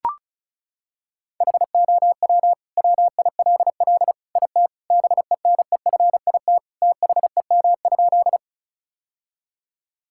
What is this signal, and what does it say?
Morse code, heard as a keyed tone, about 36 wpm, 700 Hz. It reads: HOW WILL IT BENEFIT THEM?
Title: Sentences from Top 1000 Words - Review Part 6 - 35wpm